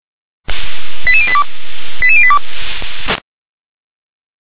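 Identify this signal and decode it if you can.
Five-tone selective call (ZVEI), heard as quick runs of five tones, tones of about 70 ms. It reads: 88082 88082
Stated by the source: ZVEI five-tone selective call received on an SDR.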